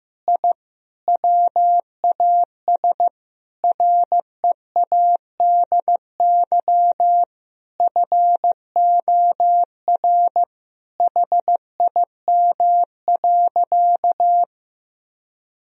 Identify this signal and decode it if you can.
Morse code, heard as a keyed tone, about 15 wpm, 700 Hz. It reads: I WAS READY FOR HIM.
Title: Sentences from Top 1000 Words - Review Part 3 - 15wpm